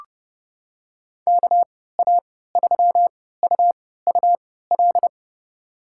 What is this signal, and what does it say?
Morse code, heard as a keyed tone, about 30 wpm, spaced as for 17 wpm, 700 Hz. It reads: KA3UUL
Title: USA Call Signs - 6 Characters - 3x Character Spacing - 30wpm